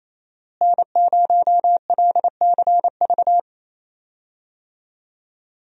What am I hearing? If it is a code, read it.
Morse code, heard as a keyed tone, about 28 wpm, 700 Hz. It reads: N0LCV
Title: USA Call Signs - 5 Characters - Repeated 3x - 28wpm